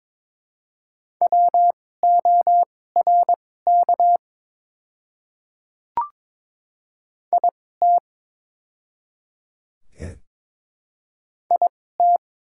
Morse code, heard as a keyed tone, about 21 wpm, spaced as for 16 wpm, 700 Hz. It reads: WORK IT IT
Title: 2x Character Spacing - Top 300 Words - 22wpm